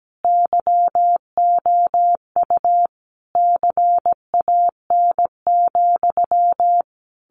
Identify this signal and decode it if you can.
Morse code, heard as a keyed tone, about 17 wpm, 700 Hz. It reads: YOU CAN,
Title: A Sound Of Thunder 17wpm